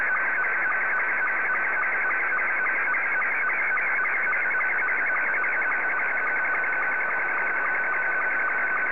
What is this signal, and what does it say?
SSTV (slow-scan television), Scottie 2.
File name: sstv / S2-VK3DEK